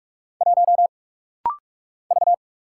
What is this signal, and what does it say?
Morse code, heard as a keyed tone, about 44 wpm, 700 Hz. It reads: 1 V
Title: Single Letter-Number - Rapid-Fire 45wpm